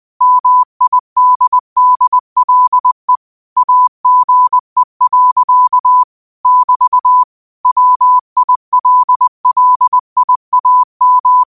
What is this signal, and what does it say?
Morse code, heard as a keyed tone, about 20 wpm, 1000 Hz. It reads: MIDDLE AGE. <BT> WILLIAM
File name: qotdfiles / 20wpm